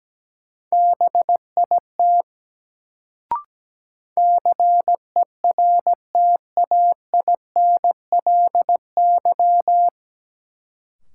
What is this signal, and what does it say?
Morse code, heard as a keyed tone, about 17 wpm, 700 Hz. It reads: BIT CERTAINLY
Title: Sentences from Top 900 Words - New Word Drill - 17wpm